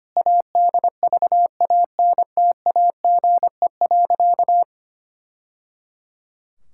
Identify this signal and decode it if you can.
Morse code, heard as a keyed tone, about 25 wpm, 700 Hz. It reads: ADVANTAGE.
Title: Sentences from Top 1000 Words - Review Part 6 - 25wpm